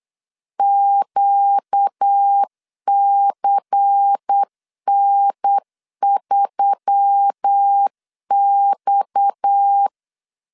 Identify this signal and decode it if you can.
Morse code, heard as a keyed tone, about 8 wpm, 800 Hz. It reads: QCN3X
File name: CW / DET / 1 6 1500C S B 8wpm 5142